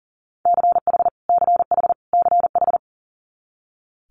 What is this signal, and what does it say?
Morse code, heard as a keyed tone, about 41 wpm, 700 Hz. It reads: CH CH CH
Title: Multi-letter Phonemes Repeated 3x 40wpm